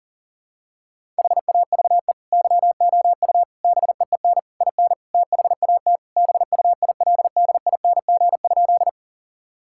Morse code, heard as a keyed tone, about 40 wpm, 700 Hz. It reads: HAVE YOU BEEN IN THAT BUILDING?
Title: Sentences from Top 1000 Words - Review Part 5 - 40wpm